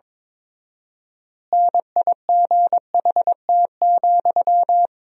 Morse code, heard as a keyed tone, about 22 wpm, 700 Hz. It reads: NIGHT,
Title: 4x Word-Spacing - Sentences from Top 300 Words - 22wpm